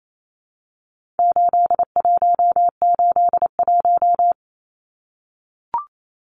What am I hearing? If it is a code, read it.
Morse code, heard as a keyed tone, about 28 wpm, 700 Hz. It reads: 8181
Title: Numbers - 4-Digit 28wpm